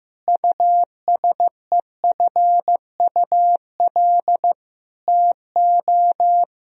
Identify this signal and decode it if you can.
Morse code, heard as a keyed tone, about 15 wpm, 700 Hz. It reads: USEFUL TO
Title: Sentences from Top 1000 Words - Review Part 4 - 15wpm